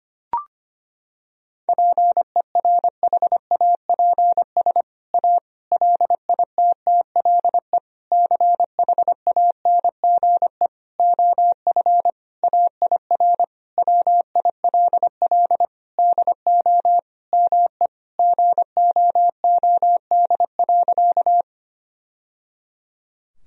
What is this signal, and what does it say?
Morse code, heard as a keyed tone, about 25 wpm, 700 Hz. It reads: PERHAPS A LITTLE CHANGE OF AIR WILL DO ME GOOD.
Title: Sentences from Top 1000 Words - Review Part 1 - 25wpm